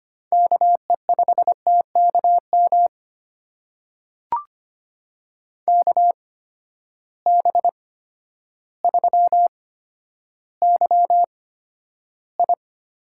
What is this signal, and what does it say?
Morse code, heard as a keyed tone, about 25 wpm, spaced as for 11 wpm, 700 Hz. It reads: KE5TKM KB3YI
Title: USA Call Signs - 6 Characters - 8x Character Spacing - 25wpm